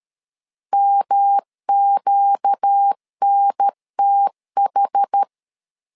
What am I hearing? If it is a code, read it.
Morse code, heard as a keyed tone, about 13 wpm, 800 Hz. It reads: MQNTH
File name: LEZIONI / 4 Q 12wpm 1833